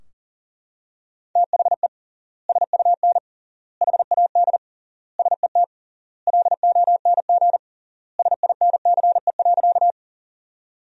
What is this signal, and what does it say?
Morse code, heard as a keyed tone, about 40 wpm, 700 Hz. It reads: THE SUN HAD SET LONG SINCE.
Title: 3x Word-Spacing - Sentences from Top 300 Words - 40wpm